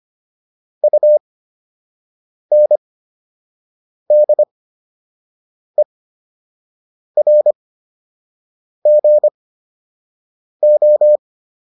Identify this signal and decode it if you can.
Morse code, heard as a keyed tone, about 25 wpm, spaced as for 6 wpm, 600 Hz. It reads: UNDERGO
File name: UN7 06